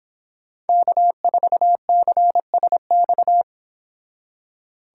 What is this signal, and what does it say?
Morse code, heard as a keyed tone, about 29 wpm, 700 Hz. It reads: K4CSX